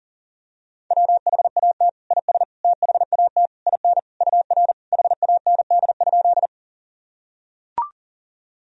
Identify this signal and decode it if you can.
Morse code, heard as a keyed tone, about 40 wpm, 700 Hz. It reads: WHAT IS THAT IN UR HAND?
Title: Sentences from Top 200 Words and Abbreviations 40wpm